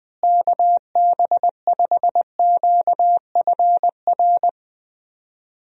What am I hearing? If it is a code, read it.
Morse code, heard as a keyed tone, about 21 wpm, 700 Hz. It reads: KB5QFR
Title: USA Call Signs - 6 Characters - Repeated 3x - 20wpm